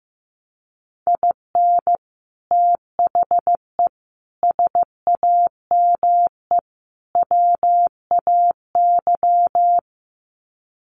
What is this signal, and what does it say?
Morse code, heard as a keyed tone, about 15 wpm, 700 Hz. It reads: IN THE SAME WAY
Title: Sets of 4 Words 15wpm